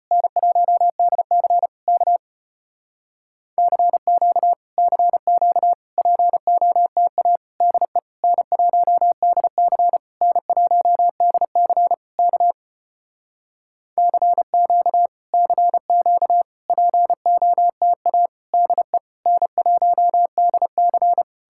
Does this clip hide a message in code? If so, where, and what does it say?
Yes: Morse code from 0.1 to 21.2 s, heard as a keyed tone, about 33 wpm, 700 Hz. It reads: N1DC K CQ CQ POTA DE N1DC N1DC K CQ CQ POTA DE N1DC